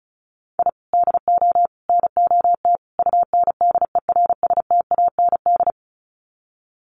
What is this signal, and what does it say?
Morse code, heard as a keyed tone, about 34 wpm, 700 Hz. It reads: I DO NOT UNDERSTAND
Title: Sets of 4 Words 35wpm